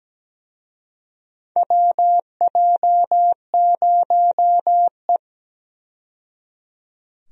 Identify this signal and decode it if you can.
Morse code, heard as a keyed tone, about 17 wpm, 700 Hz. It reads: WJ0E
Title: USA Call Signs - 4 Characters - 17wpm